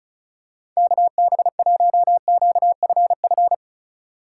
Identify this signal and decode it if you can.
Morse code, heard as a keyed tone, about 36 wpm, 700 Hz. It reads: KB1QFF
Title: USA Call Signs - 6 Characters - 4x Character Spacing - 35wpm